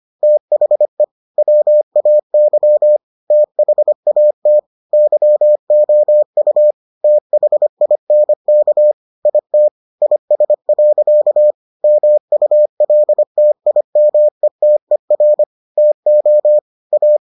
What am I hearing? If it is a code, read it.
Morse code, heard as a keyed tone, about 25 wpm, 600 Hz. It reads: THE WAY THAT YOU THINK IT IS. MULTIMETER TO A